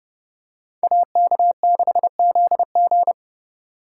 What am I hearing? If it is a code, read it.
Morse code, heard as a keyed tone, about 29 wpm, 700 Hz. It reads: AK6ZG